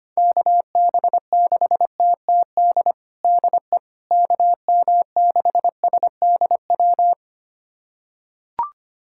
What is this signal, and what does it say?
Morse code, heard as a keyed tone, about 25 wpm, 700 Hz. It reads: KB6TTD DE KM6SDW
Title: USA Callsign DE Callsign - 6 Characters - 25wpm